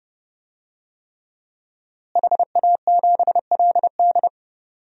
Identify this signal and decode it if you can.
Morse code, heard as a keyed tone, about 30 wpm, 700 Hz. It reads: HA7LD
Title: DXCC-100 Call Signs - 5 Characters - Repeated 3x - 30wpm